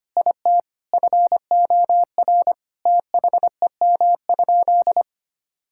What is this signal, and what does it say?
Morse code, heard as a keyed tone, about 25 wpm, 700 Hz. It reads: IT FOR THEM?
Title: Sentences from Top 1000 Words - Review Part 5 - 25wpm